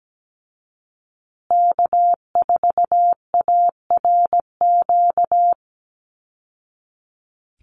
Morse code, heard as a keyed tone, about 17 wpm, 700 Hz. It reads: K4ARQ
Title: Call Signs Repeated 3x - 17wpm